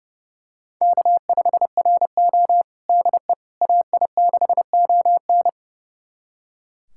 Morse code, heard as a keyed tone, about 30 wpm, 700 Hz. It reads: K5RO DE AI6ON
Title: USA Callsign DE Callsign 30wpm